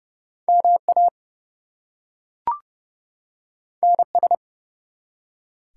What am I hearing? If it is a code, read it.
Morse code, heard as a keyed tone, about 28 wpm, 700 Hz. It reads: MA NS
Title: US States and Canadian Provinces 30wpm